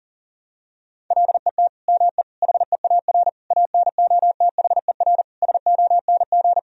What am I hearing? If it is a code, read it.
Morse code, heard as a keyed tone, about 40 wpm, 700 Hz. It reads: LET ME HEAR ANOTHER SONG
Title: Sentences from Top 1000 Words - Review Part 2 - 40wpm